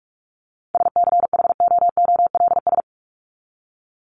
Morse code, heard as a keyed tone, about 44 wpm, 700 Hz. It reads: SCHOOLS
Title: ES:S Suffix - Top 100 Words with Base Words 45wpm